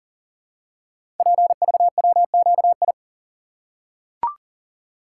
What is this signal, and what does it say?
Morse code, heard as a keyed tone, about 38 wpm, 700 Hz. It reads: PVWQI
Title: Code Groups - 5 Letters - 40wpm